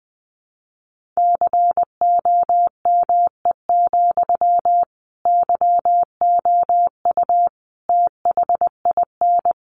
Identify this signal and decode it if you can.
Morse code, heard as a keyed tone, about 20 wpm, 700 Hz. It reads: COME, YOU THIN
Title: Sentences from Top 1200 Words 20wpm